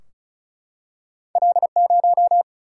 Morse code, heard as a keyed tone, about 36 wpm, 700 Hz. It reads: L 0
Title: Single-Letter Single-Number 35wpm